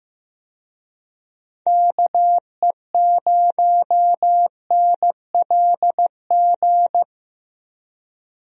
Morse code, heard as a keyed tone, about 15 wpm, 700 Hz. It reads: KE0NLG
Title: USA Call Signs - 6 Characters - 15wpm